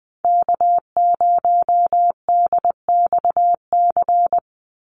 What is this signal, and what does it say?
Morse code, heard as a keyed tone, about 20 wpm, 700 Hz. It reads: K0DXC
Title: Call Signs Repeated 3x - 20wpm